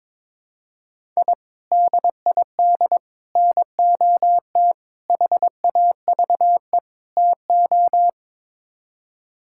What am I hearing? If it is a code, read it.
Morse code, heard as a keyed tone, about 22 wpm, 700 Hz. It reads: I DID NOT HAVE TO
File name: practice / 22wpm / Build A Phrase - 5-Word Phrases - 22wpm